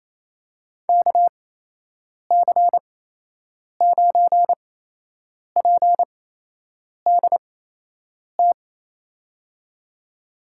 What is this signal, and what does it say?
Morse code, heard as a keyed tone, about 28 wpm, spaced as for 8 wpm, 700 Hz. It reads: KC9PDT
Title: USA Call Signs - 6 Characters - 8x Character Spacing - 28wpm